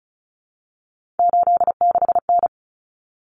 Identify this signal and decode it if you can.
Morse code, heard as a keyed tone, about 34 wpm, 700 Hz. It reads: 86N
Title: Common Cut Numbers - 3-Digit 35wpm